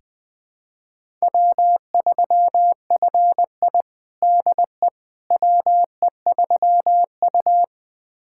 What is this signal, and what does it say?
Morse code, heard as a keyed tone, about 20 wpm, 700 Hz. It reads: W3FI DE WE3U
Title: USA Callsign DE Callsign - 4 Characters - 20wpm